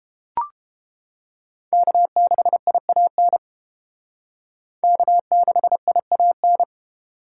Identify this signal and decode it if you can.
Morse code, heard as a keyed tone, about 28 wpm, 700 Hz. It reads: K6IAN K6IAN